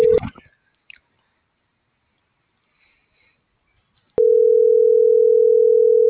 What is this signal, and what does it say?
Ringback tone (North America), 440 + 480 Hz.